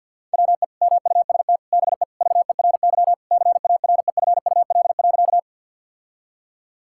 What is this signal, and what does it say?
Morse code, heard as a keyed tone, about 50 wpm, 700 Hz. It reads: WE MUST BE VERY CAREFUL.
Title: Sentences from Top 1600 Words 50wpm